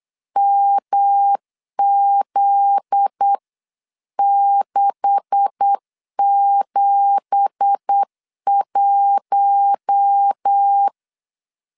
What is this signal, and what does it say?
Morse code, heard as a keyed tone, about 8 wpm, 800 Hz. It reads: MZ671
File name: 8wpm